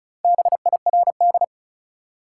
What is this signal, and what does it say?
Morse code, heard as a keyed tone, about 36 wpm, 700 Hz. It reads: BIRD
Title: Four Letter Words - 35wpm